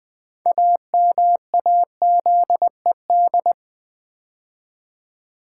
Morse code, heard as a keyed tone, about 20 wpm, 700 Hz. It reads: AMAZED
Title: Build A Word - 2 Letters At A Time - 6-Letter Words - 20wpm